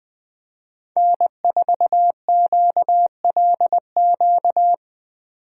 Morse code, heard as a keyed tone, about 21 wpm, 700 Hz. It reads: N4QLQ